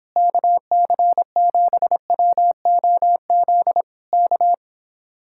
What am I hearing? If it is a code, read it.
Morse code, heard as a keyed tone, about 26 wpm, 700 Hz. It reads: KC7WOZ K
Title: Speed-Racing Common QSO Phrases Encore - 22wpm